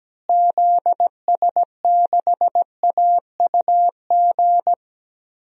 Morse code, heard as a keyed tone, about 17 wpm, 700 Hz. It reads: ZS6AUG
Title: DXCC-100 Call Signs - 6 Characters - Repeated 2x - 17wpm